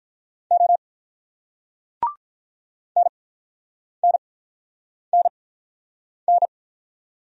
Morse code, heard as a keyed tone, about 47 wpm, spaced as for 9 wpm, 700 Hz. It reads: O NNNN